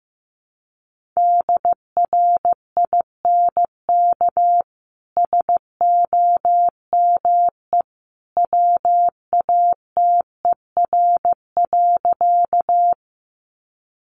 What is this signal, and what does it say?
Morse code, heard as a keyed tone, about 15 wpm, 700 Hz. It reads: DRINK SOME WATER.
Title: Sentences from Top 1500 Words 15wpm